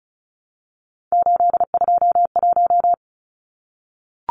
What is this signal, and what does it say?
Morse code, heard as a keyed tone, about 34 wpm, 700 Hz. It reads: 821
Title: Numbers - 3-Digit 35wpm